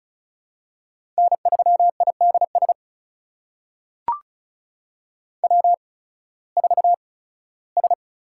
Morse code, heard as a keyed tone, about 34 wpm, spaced as for 18 wpm, 700 Hz. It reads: N3IDS W4S
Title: USA Call Signs - 5 Characters - 8x Character Spacing - 35wpm